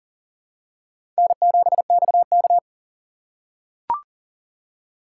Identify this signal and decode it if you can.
Morse code, heard as a keyed tone, about 40 wpm, 700 Hz. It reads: N7XK
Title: USA Call Signs - 4 Characters - 3x Character Spacing - 40wpm